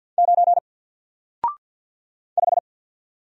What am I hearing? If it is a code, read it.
Morse code, heard as a keyed tone, about 48 wpm, 700 Hz. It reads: 9 5
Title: Single Letter-Number Alt 50wpm